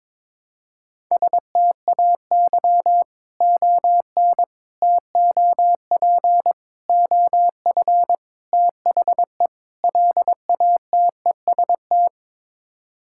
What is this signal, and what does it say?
Morse code, heard as a keyed tone, about 22 wpm, 700 Hz. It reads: STAY ON TOP OF THE LATEST
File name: practice / 22wpm / Sets of 6 Words - 22wpm